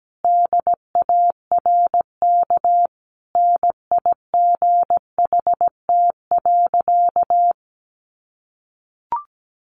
Morse code, heard as a keyed tone, about 17 wpm, 700 Hz. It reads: DARK NIGHT.
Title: Sentences from Top 500 Words 17wpm